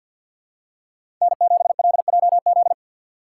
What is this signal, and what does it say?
Morse code, heard as a keyed tone, about 52 wpm, 700 Hz. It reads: N7LJZ
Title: USA Call Signs - 2x Character Spacing - 50wpm